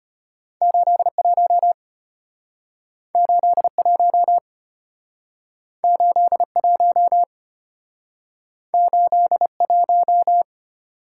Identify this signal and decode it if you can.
Morse code, heard as a keyed tone, about 33 wpm, 700 Hz. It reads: 81 81 81 81